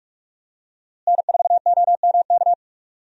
Morse code, heard as a keyed tone, about 46 wpm, 700 Hz. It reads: N4OMK
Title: USA Call Signs - 5 Characters - Repeated 3x - 45wpm